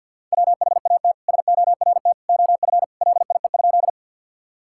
Morse code, heard as a keyed tone, about 50 wpm, 700 Hz. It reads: WHAT SORT OF LIE?
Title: Sentences from Top 1600 Words 50wpm